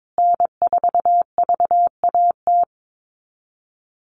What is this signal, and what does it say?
Morse code, heard as a keyed tone, about 22 wpm, 700 Hz. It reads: N4VAT
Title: USA Call Signs Repeated 3x 22wpm